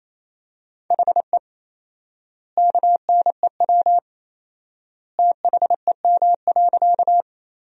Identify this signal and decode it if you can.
Morse code, heard as a keyed tone, about 27 wpm, 700 Hz. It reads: HE KNEW THEM.
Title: 4x Word-Spacing - Sentences from Top 400 Words - 28wpm